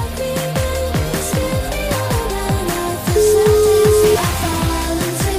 Ringback tone (Europe, 425 Hz), with noise.